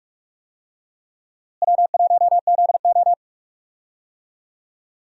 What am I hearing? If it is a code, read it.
Morse code, heard as a keyed tone, about 43 wpm, 700 Hz. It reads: W1ZO
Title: USA Call Signs - 4 Characters - Repeated 2x - 45wpm